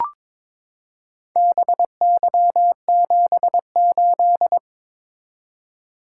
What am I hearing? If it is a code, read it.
Morse code, heard as a keyed tone, about 21 wpm, 700 Hz. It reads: BY78
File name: practice / 22wpm / Four Letter-Number Combinations - 22wpm